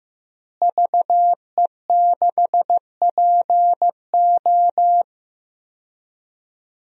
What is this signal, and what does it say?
Morse code, heard as a keyed tone, about 15 wpm, 700 Hz. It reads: VE6PO
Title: DXCC-100 Call Signs - 5 Characters - Repeated 3x - 15wpm